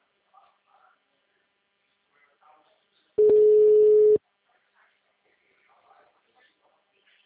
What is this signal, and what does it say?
Ringback tone (Europe, 425 Hz), with speech.